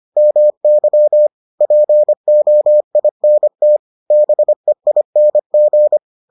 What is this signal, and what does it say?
Morse code, heard as a keyed tone, about 25 wpm, 600 Hz. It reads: MY POINT BEING